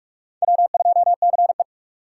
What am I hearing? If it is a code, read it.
Morse code, heard as a keyed tone, about 41 wpm, 700 Hz. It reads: W2CE
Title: USA Call Signs - 2x Character Spacing - 45wpm